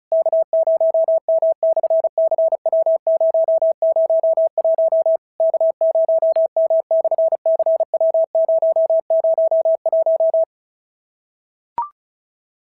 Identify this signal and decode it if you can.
Morse code, heard as a keyed tone, about 36 wpm, 650 Hz. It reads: K0M/CW001 K0M/CW001
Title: SOTA QSO Practice 35wpm